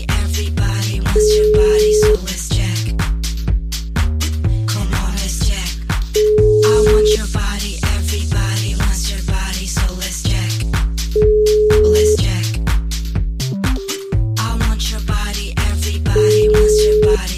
Ringback tone (Europe, 425 Hz), with noise.